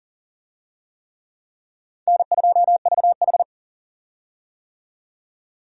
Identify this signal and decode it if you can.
Morse code, heard as a keyed tone, about 40 wpm, 700 Hz. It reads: N2VH
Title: USA Call Signs - 4 Characters - 40wpm